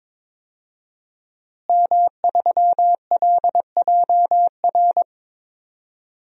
Morse code, heard as a keyed tone, about 22 wpm, 700 Hz. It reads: M3LJR
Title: DXCC-100 Call Signs - 5 Characters - Repeated 3x - 22wpm